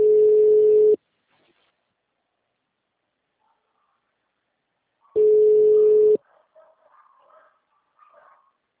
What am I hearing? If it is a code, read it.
Ringback tone (Europe, 425 Hz).
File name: out- -2017- - - .5466